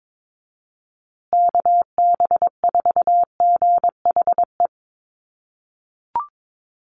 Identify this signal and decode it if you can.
Morse code, heard as a keyed tone, about 22 wpm, 690 Hz. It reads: KB4GHE
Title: USA Call Signs Repeated 3x 22wpm